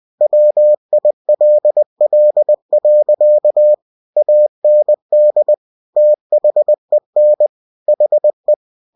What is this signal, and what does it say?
Morse code, heard as a keyed tone, about 20 wpm, 600 Hz. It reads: WILL. AND THEN HE
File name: cw-books / winnie-the-pooh / winnie-20wpm-page-0034